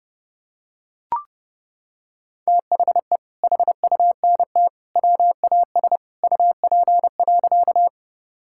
Morse code, heard as a keyed tone, about 30 wpm, 700 Hz. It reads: THE HUNT WAS UP.
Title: Sentences from Top 1600 Words 30wpm